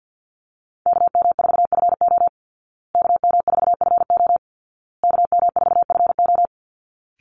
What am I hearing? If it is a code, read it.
Morse code, heard as a keyed tone, about 50 wpm, 700 Hz. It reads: KM4FO KM4FO KM4FO
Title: Call Signs Repeated 2x - 50wpm